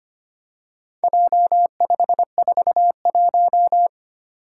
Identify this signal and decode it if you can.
Morse code, heard as a keyed tone, about 25 wpm, 700 Hz. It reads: J541
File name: practice / 25wpm / Four Letter-Number Combinations - 25wpm